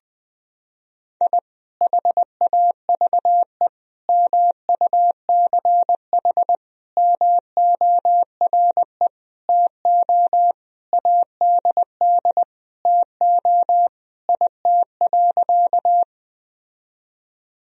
Morse code, heard as a keyed tone, about 20 wpm, 700 Hz. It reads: I HAVE MUCH MORE TO ADD TO IT.
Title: Sentences from Top 1000 Words - Review Part 1 - 20wpm